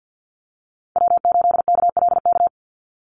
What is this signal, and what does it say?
Morse code, heard as a keyed tone, about 50 wpm, 700 Hz. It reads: W8KLK
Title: USA Call Signs 50wpm